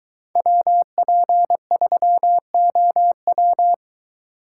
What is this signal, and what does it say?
Morse code, heard as a keyed tone, about 23 wpm, 700 Hz. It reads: WP3OW